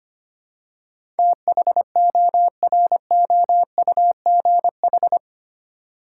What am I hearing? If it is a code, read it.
Morse code, heard as a keyed tone, about 25 wpm, 700 Hz. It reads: THOROUGH
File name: Eight Letter Words - 25wpm